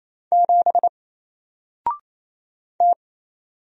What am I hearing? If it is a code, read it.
Morse code, heard as a keyed tone, about 27 wpm, 700 Hz. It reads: 7 T
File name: Single-Letter-Number-28wpm